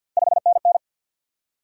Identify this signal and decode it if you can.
Morse code, heard as a keyed tone, about 53 wpm, 700 Hz. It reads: 5NN
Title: Top 90 QSO Elements 50wpm